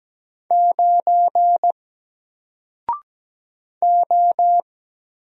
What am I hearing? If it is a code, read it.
Morse code, heard as a keyed tone, about 17 wpm, 700 Hz. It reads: 9 O
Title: Single Letter-Number 17wpm